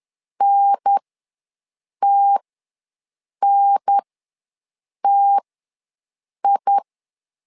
Morse code, heard as a keyed tone, about 10 wpm, spaced as for 6 wpm, 800 Hz. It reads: NTNTI
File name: 2 UHDN ws 10wpm